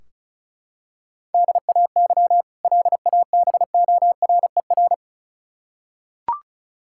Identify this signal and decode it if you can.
Morse code, heard as a keyed tone, about 34 wpm, 700 Hz. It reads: DAY LABORER
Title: K1USN SST QSO Practice 35wpm